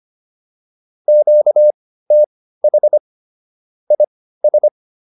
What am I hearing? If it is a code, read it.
Morse code, heard as a keyed tone, about 25 wpm, spaced as for 15 wpm, 600 Hz. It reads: QTH IS